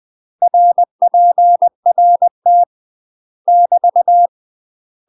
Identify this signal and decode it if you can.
Morse code, heard as a keyed tone, about 20 wpm, 700 Hz. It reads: RPRT <BT>
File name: UK MASTER QSOs TRAINING / 20WPM / UK MASTER QSOs 0030